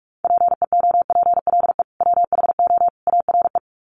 Morse code, heard as a keyed tone, about 44 wpm, 700 Hz. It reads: PEOPLE WHO ARE
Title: Sets of 3 Words 45wpm